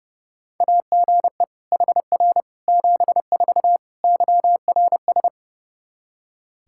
Morse code, heard as a keyed tone, about 30 wpm, 700 Hz. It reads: AGE HR 74 YRS
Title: Practice QSO - Exchange 3 - More Info - 30wpm